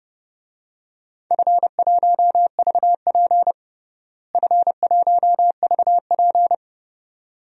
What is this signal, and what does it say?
Morse code, heard as a keyed tone, about 32 wpm, 700 Hz. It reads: F1VP F1VP
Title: DXCC-100 Call Signs - 4 Characters - Repeated 2x - 30wpm